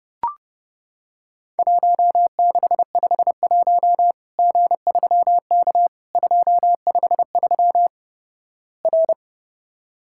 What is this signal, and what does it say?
Morse code, heard as a keyed tone, about 30 wpm, 700 Hz. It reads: 1651 G3K 253 R
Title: Worked All Europe DX Contest - QTC Exchange - 30wpm